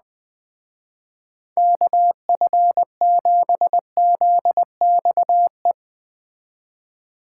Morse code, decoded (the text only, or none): KF7ZXE